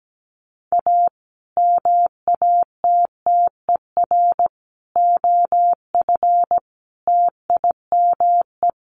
Morse code, heard as a keyed tone, about 17 wpm, 700 Hz. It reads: A MATTER OF TIME
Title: Sets of 5 Words Encore 17wpm